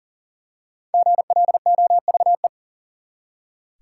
Morse code, heard as a keyed tone, about 39 wpm, 700 Hz. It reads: GLOVE
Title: Five Letter Words - 40wpm